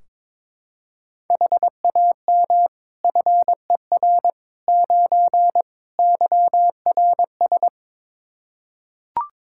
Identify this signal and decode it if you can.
Morse code, heard as a keyed tone, about 22 wpm, 700 Hz. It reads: HAM FER 9 YRS
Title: K1USN SST QSO Practice 22wpm